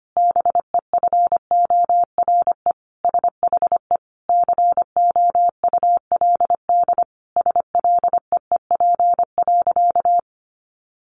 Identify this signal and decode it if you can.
Morse code, heard as a keyed tone, about 25 wpm, 700 Hz. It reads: BEFORE SHE COULD SLEEP.
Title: Sentences from Top 500 Words 25wpm